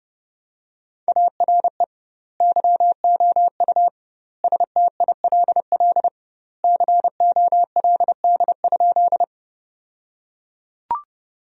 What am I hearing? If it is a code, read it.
Morse code, heard as a keyed tone, about 31 wpm, 700 Hz. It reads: ARE YOU STILL COLD?
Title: 2x Word-Spacing - Sentences from Top 400 Words - 30wpm